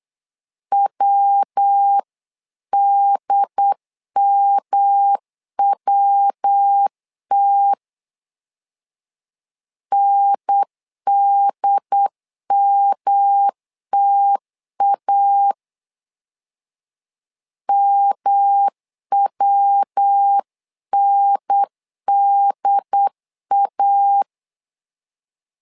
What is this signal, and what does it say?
Morse code, heard as a keyed tone, about 8 wpm, 800 Hz. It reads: WDMWT NDMTA MWNDA
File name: LEZIONI / 3 MWRO / 3 MW 8wpm 2501